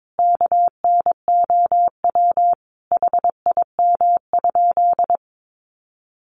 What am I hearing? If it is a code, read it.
Morse code, heard as a keyed tone, about 22 wpm, 700 Hz. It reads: KNOW HIM?
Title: Sentences from Top 100 Words 22wpm